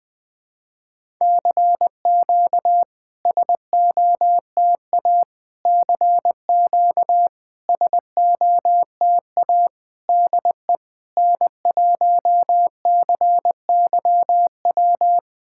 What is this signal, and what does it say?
Morse code, heard as a keyed tone, about 20 wpm, 690 Hz. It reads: CQ SOTA CQ SOTA DE N1CYW
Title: SOTA QSO Practice 20wpm